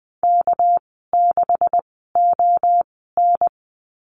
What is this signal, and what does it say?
Morse code, heard as a keyed tone, about 20 wpm, spaced as for 15 wpm, 700 Hz. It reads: K6ON
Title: 245 Lesson 39g - Letter z - Call Signs 20wpm